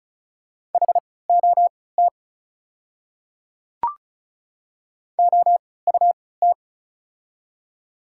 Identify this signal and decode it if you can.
Morse code, heard as a keyed tone, about 34 wpm, spaced as for 20 wpm, 690 Hz. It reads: HOT OUT
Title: 3x Character Spacing - Top 100 Words - 35wpm